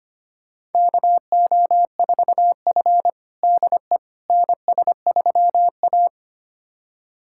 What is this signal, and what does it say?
Morse code, heard as a keyed tone, about 25 wpm, 700 Hz. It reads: KO4F DE NS3A
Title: USA Callsign DE Callsign - 4 Characters - 25wpm